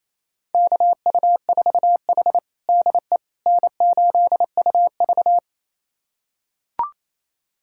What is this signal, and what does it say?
Morse code, heard as a keyed tone, about 28 wpm, 700 Hz. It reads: KU4H DE N8UV
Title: USA Callsign DE Callsign - 4 Characters - 28wpm